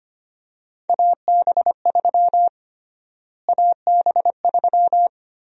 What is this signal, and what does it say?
Morse code, heard as a keyed tone, about 25 wpm, 700 Hz. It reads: AB3 AB3
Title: USA Call Sign Prefixes - Repeated 2x - 25wpm